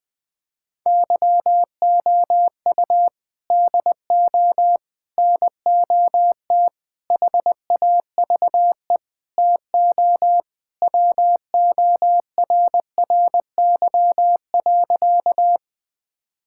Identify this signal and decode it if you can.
Morse code, heard as a keyed tone, about 20 wpm, 700 Hz. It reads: YOU DO NOT HAVE TO WORRY.
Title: Sentences with 5-Word Phrases - 20wpm